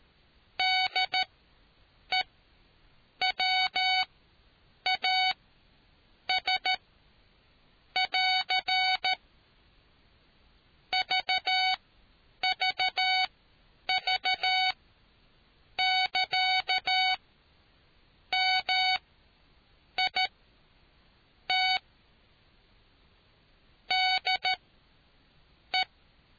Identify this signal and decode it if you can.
Morse code, heard as a keyed tone, about 13 wpm, spaced as for 7 wpm, 750 Hz. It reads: DE WAS <AR> VVV <KA> MIT DE